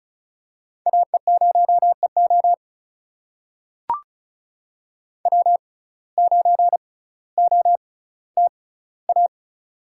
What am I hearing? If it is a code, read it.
Morse code, heard as a keyed tone, about 36 wpm, spaced as for 18 wpm, 700 Hz. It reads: AE0EO W9OTA
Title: USA Call Signs - 5 Characters - 6x Character Spacing - 35wpm